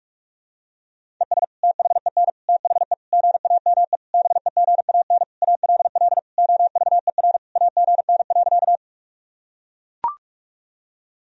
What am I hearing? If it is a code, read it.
Morse code, heard as a keyed tone, about 41 wpm, 700 Hz. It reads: ES THEN THE GAME BEGAN ALL OVER AGN.
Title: Sentences from Top 500 Words and Abbreviations 45wpm